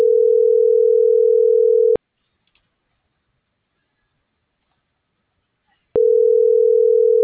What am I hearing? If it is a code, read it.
Ringback tone (North America), 440 + 480 Hz.